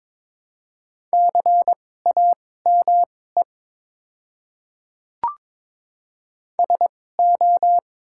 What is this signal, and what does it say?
Morse code, heard as a keyed tone, about 22 wpm, spaced as for 16 wpm, 700 Hz. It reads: CAME SO
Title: 2x Character Spacing - Top 200 Words - 22wpm